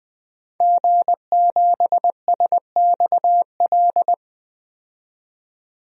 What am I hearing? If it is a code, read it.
Morse code, heard as a keyed tone, about 20 wpm, 700 Hz. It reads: G7SXL
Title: DXCC-100 Call Signs - 5 Characters - 20wpm